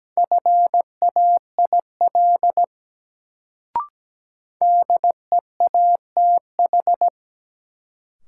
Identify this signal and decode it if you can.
Morse code, heard as a keyed tone, about 17 wpm, 700 Hz. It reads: FAIL DEATH
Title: Sentences from Top 1100 Words - New Word Drill Alt - 17wpm